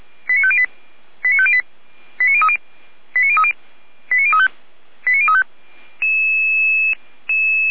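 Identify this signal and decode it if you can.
Five-tone selective call (ZVEI), heard as quick runs of five tones, tones of about 70 ms. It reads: 89508 89508 89030 89030 89035 89035
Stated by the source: ZVEI five-tone call received by SDR.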